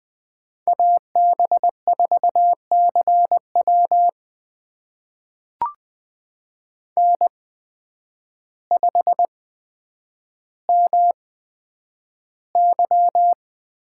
Morse code, heard as a keyed tone, about 20 wpm, spaced as for 9 wpm, 700 Hz. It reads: AB4CW N5MY
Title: USA Call Signs - 8x Character Spacing - 20wpm